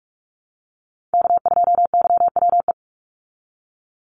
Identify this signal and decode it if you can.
Morse code, heard as a keyed tone, about 44 wpm, 700 Hz. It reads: K2YWE
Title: Call Signs Repeated 2x - 45wpm